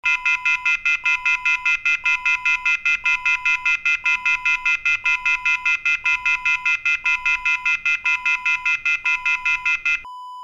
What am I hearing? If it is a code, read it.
Off-hook warning tone, 1400 + 2060 + 2450 + 2600 Hz, with noise.
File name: pub / phone-off-hook-preview-1